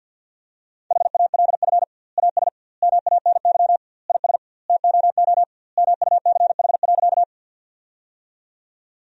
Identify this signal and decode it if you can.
Morse code, heard as a keyed tone, about 50 wpm, 700 Hz. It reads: HALF AS MANY IS TOO MUCH.
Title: 2x Word-Spacing - Sentences from Top 400 Words - 50wpm